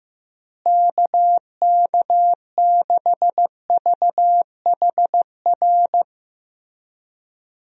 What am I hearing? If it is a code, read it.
Morse code, heard as a keyed tone, about 16 wpm, 700 Hz. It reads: KK6VHR
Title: USA Call Signs - 6 Characters - Repeated 3x - 15wpm